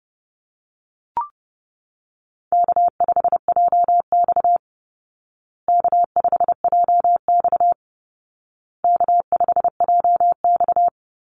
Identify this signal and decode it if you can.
Morse code, heard as a keyed tone, about 30 wpm, 700 Hz. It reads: K5JX K5JX K5JX
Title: Call Signs Repeated 2x - 30wpm